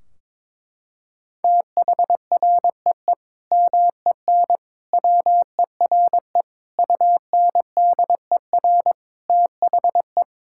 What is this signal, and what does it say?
Morse code, heard as a keyed tone, about 22 wpm, 700 Hz. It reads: THREE MEN WERE UNDER THE
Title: Sentences from Top 1600 Words 22wpm